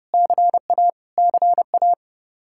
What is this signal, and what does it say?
Morse code, heard as a keyed tone, about 30 wpm, 700 Hz. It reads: CA CA
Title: Practice QSO - Exchange 1 - Beginning - 30wpm